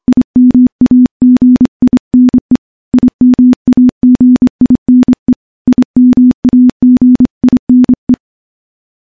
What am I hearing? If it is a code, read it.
Morse code, heard as a keyed tone, about 26 wpm, 250 Hz. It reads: IMAGINE IMAGINE IMAGINE
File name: morse-training-25WPM